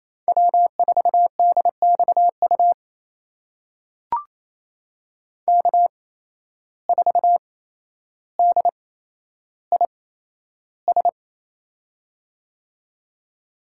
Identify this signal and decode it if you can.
Morse code, heard as a keyed tone, about 27 wpm, spaced as for 12 wpm, 700 Hz. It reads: W4DXU K4DIS
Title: USA Call Signs - 5 Characters - 8x Character Spacing - 28wpm